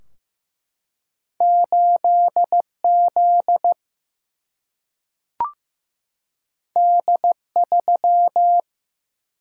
Morse code, heard as a keyed tone, about 15 wpm, 700 Hz. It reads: 8Z D3
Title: Single-Letter Single-Number 15wpm